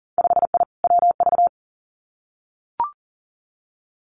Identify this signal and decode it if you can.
Morse code, heard as a keyed tone, about 40 wpm, 700 Hz. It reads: 5I WV
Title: Winter Field Day Exchanges Encore 40wpm